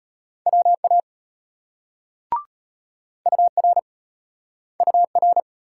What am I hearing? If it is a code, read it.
Morse code, heard as a keyed tone, about 34 wpm, 700 Hz. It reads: WA UR UR